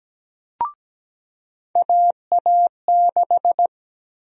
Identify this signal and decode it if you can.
Morse code, heard as a keyed tone, about 17 wpm, 700 Hz. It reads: AA6
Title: USA Call Sign Prefixes - 17wpm